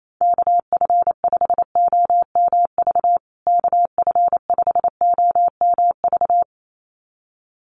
Morse code, heard as a keyed tone, about 28 wpm, 700 Hz. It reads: KF5OMV KF5OMV
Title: USA Call Signs Repeated 2x 28wpm